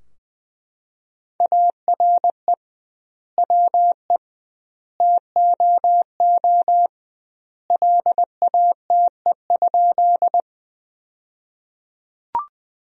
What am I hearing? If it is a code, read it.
Morse code, heard as a keyed tone, about 21 wpm, 700 Hz. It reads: ARE WE TOO LATE?
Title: 2x Word-Spacing - Sentences from Top 300 Words - 20wpm